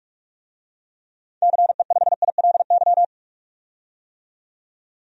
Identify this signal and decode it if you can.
Morse code, heard as a keyed tone, about 41 wpm, 700 Hz. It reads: CE5ILY
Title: DXCC-100 Call Signs - 6 Characters - 45wpm